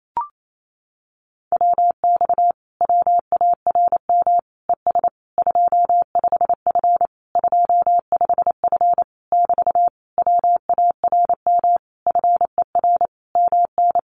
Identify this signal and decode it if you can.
Morse code, heard as a keyed tone, about 28 wpm, 700 Hz. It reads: WX WARM ES 25F 25F <BT> WARM FER MN
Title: Common QSO Phrases Encore 28wpm